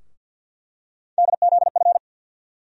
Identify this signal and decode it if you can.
Morse code, heard as a keyed tone, about 48 wpm, 700 Hz. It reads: DZF
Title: USA Call Sign Postfixes - 3 Letters 50wpm